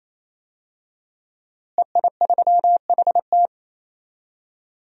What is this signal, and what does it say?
Morse code, heard as a keyed tone, about 28 wpm, 700 Hz. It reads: EI3HT
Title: DXCC-100 Call Signs - 5 Characters - Repeated 3x - 28wpm